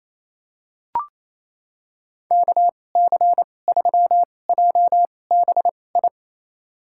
Morse code, heard as a keyed tone, about 28 wpm, spaced as for 20 wpm, 700 Hz. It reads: KC3JBI
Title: USA Call Signs - 6 Characters - 2x Character Spacing - 28wpm